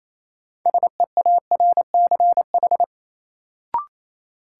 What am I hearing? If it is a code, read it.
Morse code, heard as a keyed tone, about 28 wpm, 700 Hz. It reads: SEARCH
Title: Sentences from Top 800 Words - New Word Drill - 28wpm